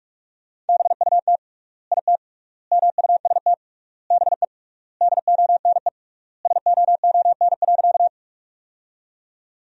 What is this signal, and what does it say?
Morse code, heard as a keyed tone, about 46 wpm, 700 Hz. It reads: BUT IT MUST BE DONE SOON.
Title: 3x Word-Spacing - Sentences from Top 500 Words - 45wpm